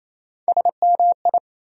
Morse code, heard as a keyed tone, about 28 wpm, 700 Hz. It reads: SMI
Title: USA Call Sign Postfixes - 3 Letters 28wpm